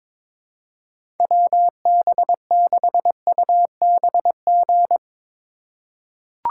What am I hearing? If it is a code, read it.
Morse code, heard as a keyed tone, about 21 wpm, 700 Hz. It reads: WB6UBG
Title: USA Call Signs - 6 Characters - 8x Character Spacing - 22wpm